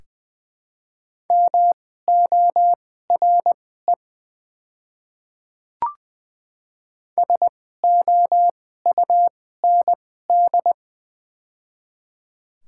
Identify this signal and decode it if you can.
Morse code, heard as a keyed tone, about 20 wpm, spaced as for 14 wpm, 700 Hz. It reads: MORE SOUND
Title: 2x Character Spacing - Top 200 Words - 20wpm